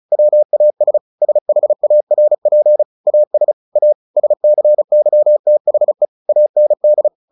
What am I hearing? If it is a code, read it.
Morse code, heard as a keyed tone, about 34 wpm, 600 Hz. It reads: WAS SHARP AS A SCYTHE AND
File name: the old man and the sea / the old man and the sea 1 35wpm